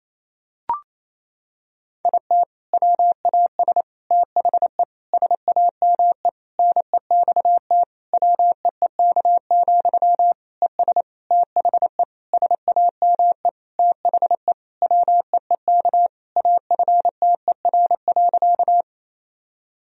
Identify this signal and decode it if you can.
Morse code, heard as a keyed tone, about 28 wpm, 700 Hz. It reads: IT WAS THE SAME NEXT WEEK, ES THE SAME THE WEEK AFTER.
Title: Sentences from Top 500 Words and Abbreviations 28wpm